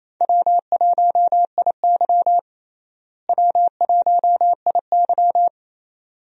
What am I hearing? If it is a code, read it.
Morse code, heard as a keyed tone, about 27 wpm, 700 Hz. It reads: W1IY W1IY
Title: USA Call Signs - 4 Characters - Repeated 2x - 28wpm